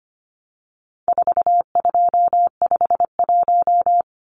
Morse code, heard as a keyed tone, about 25 wpm, 700 Hz. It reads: 4251
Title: Numbers - 4-Digit 25wpm